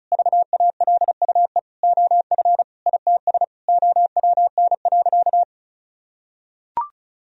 Morse code, heard as a keyed tone, about 36 wpm, 700 Hz. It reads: VALUE OF ITS OWN.
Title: Sentences from Top 1000 Words - Review Part 4 - 35wpm